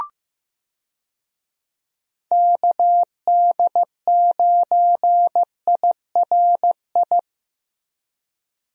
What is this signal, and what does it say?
Morse code, heard as a keyed tone, about 15 wpm, 700 Hz. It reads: KD9IRI
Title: USA Call Signs - 6 Characters - 15wpm